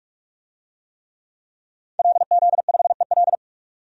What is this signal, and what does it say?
Morse code, heard as a keyed tone, about 46 wpm, 700 Hz. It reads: LZ5EL
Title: DXCC-100 Call Signs - 5 Characters - 45wpm